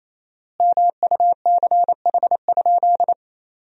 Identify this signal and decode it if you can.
Morse code, heard as a keyed tone, about 28 wpm, 700 Hz. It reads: MUCH?
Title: 4x Word-Spacing - Sentences from Top 400 Words - 28wpm